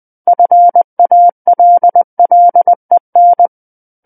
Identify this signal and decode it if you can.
Morse code, heard as a keyed tone, about 20 wpm, 700 Hz. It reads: FALLEN
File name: 20 WPM